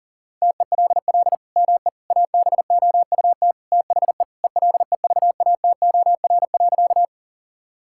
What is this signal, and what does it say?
Morse code, heard as a keyed tone, about 37 wpm, 700 Hz. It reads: TELL ME ABOUT THE ELEVATOR.
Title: Sentences from Top 1600 Words 40wpm